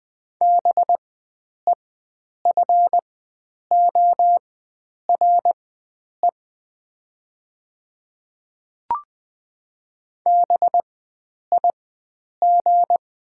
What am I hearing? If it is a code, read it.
Morse code, heard as a keyed tone, about 20 wpm, spaced as for 9 wpm, 700 Hz. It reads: BEFORE BIG
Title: 4x Character Spacing - Top 300 Words - 20wpm